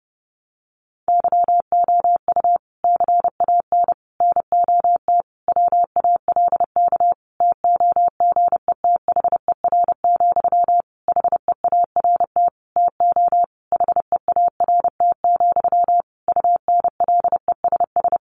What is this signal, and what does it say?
Morse code, heard as a keyed tone, about 30 wpm, 700 Hz. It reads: YOU CAN NOT WALK TOGETHER, HEART TO HEART, UNLESS
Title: Sentences from Top 1500 Words 30wpm